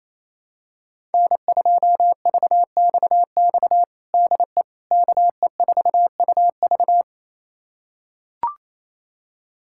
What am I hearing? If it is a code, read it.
Morse code, heard as a keyed tone, about 28 wpm, 700 Hz. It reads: N2VXX DE KE4UV
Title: USA Callsign DE Callsign - 5 Characters - 28wpm